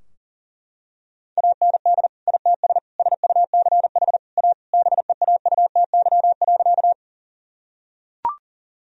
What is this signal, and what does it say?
Morse code, heard as a keyed tone, about 40 wpm, 700 Hz. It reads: AND ITS SUCH A BEAUTY.
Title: Sentences from Top 1000 Words - Review Part 2 - 40wpm